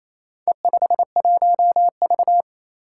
Morse code, heard as a keyed tone, about 27 wpm, 700 Hz. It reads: E51V
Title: Four Challenging Letter-Number Combinations 28wpm